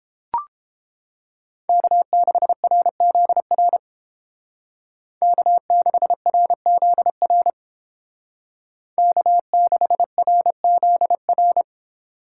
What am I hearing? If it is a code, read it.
Morse code, heard as a keyed tone, about 31 wpm, 700 Hz. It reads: K6RZR K6RZR K6RZR